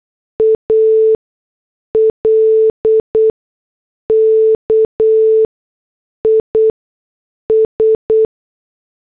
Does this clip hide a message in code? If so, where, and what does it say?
Yes: Morse code from 0.4 to 8.3 s, heard as a keyed tone, about 8 wpm, spaced as for 6 wpm, 440 Hz. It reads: ALK IS